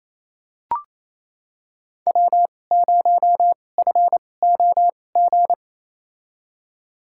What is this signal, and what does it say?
Morse code, heard as a keyed tone, about 28 wpm, spaced as for 20 wpm, 700 Hz. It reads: W0FOG
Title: USA Call Signs - 5 Characters - 2x Character Spacing - 28wpm